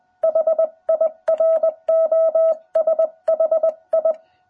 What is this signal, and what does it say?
Morse code, heard as a keyed tone, about 20 wpm, 660 Hz. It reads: HIROSHI
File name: jh8cbh / 2021cw